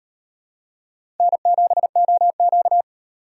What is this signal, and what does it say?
Morse code, heard as a keyed tone, about 38 wpm, 700 Hz. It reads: N7OQ